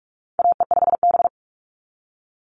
Morse code, heard as a keyed tone, about 46 wpm, 690 Hz. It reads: AE5B
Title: NA SOTA Chaser Callsigns 45wpm